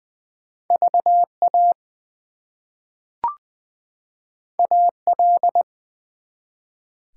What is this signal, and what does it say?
Morse code, heard as a keyed tone, about 20 wpm, 700 Hz. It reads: VA AL
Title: US States and Canadian Provinces 20wpm